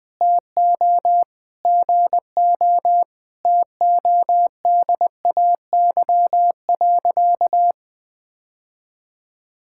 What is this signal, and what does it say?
Morse code, heard as a keyed tone, about 20 wpm, 700 Hz. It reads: TO GO TODAY.
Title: Sentences with 5-Word Phrases - 20wpm